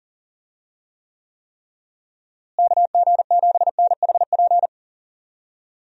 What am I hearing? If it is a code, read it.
Morse code, heard as a keyed tone, about 40 wpm, 700 Hz. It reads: KG7NHP
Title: USA Call Signs - 6 Characters - Repeated 3x - 40wpm